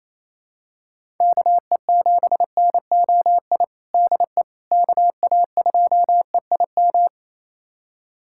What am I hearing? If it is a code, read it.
Morse code, heard as a keyed tone, about 27 wpm, 700 Hz. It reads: KE7NOI DE KA2EIM
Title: USA Callsign DE Callsign - 6 Characters - 28wpm